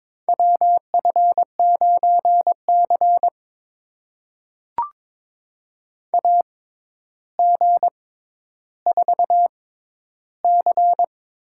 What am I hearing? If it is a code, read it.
Morse code, heard as a keyed tone, about 22 wpm, spaced as for 11 wpm, 700 Hz. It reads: WF9C AG4C
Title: USA Call Signs - 4 Characters - 6x Character Spacing - 22wpm